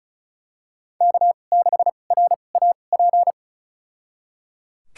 Morse code, heard as a keyed tone, about 34 wpm, spaced as for 25 wpm, 700 Hz. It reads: K6RAP
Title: USA Call Signs - 5 Characters - 2x Character Spacing - 35wpm